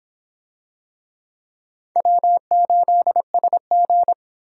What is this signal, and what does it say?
Morse code, heard as a keyed tone, about 26 wpm, 700 Hz. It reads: W8SG